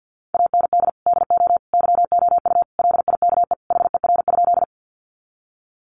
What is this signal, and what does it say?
Morse code, heard as a keyed tone, about 50 wpm, 700 Hz. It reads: AND DO YOU LIKE HER?
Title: Sentences from Top 100 Words 50wpm